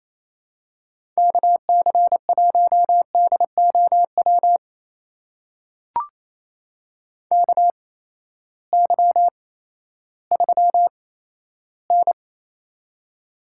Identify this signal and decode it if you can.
Morse code, heard as a keyed tone, about 28 wpm, spaced as for 14 wpm, 700 Hz. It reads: KC1DOW KY3N
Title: USA Call Signs - 8x Character Spacing - 28wpm